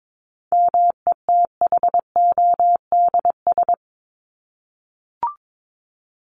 Morse code, heard as a keyed tone, about 22 wpm, 700 Hz. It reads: METHODS
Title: ES:S Suffix - Top 100 Words with Base Words 22wpm